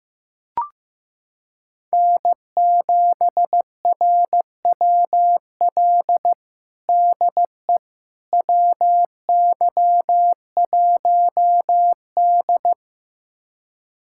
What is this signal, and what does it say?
Morse code, heard as a keyed tone, about 15 wpm, 700 Hz. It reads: N7RWL DE WY1D
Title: USA Callsign DE Callsign 15wpm